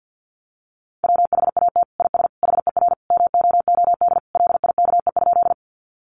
Morse code, heard as a keyed tone, about 49 wpm, 700 Hz. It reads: WHAT IS HER MOOD LIKE?
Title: Sentences from Top 1100 Words 50wpm